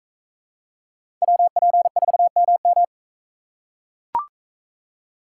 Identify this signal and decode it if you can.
Morse code, heard as a keyed tone, about 42 wpm, 700 Hz. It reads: WP4MM